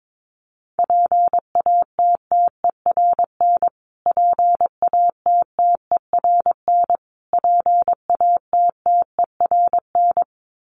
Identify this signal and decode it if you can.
Morse code, heard as a keyed tone, about 22 wpm, 700 Hz. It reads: PATTERN PATTERN PATTERN
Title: Top 500 Words Repeated 22wpm